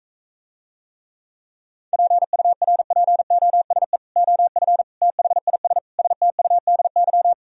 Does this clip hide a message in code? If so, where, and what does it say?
Yes: Morse code from 1.9 to 7.3 s, heard as a keyed tone, about 44 wpm, 700 Hz. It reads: PURPOSE OF THIS STUDY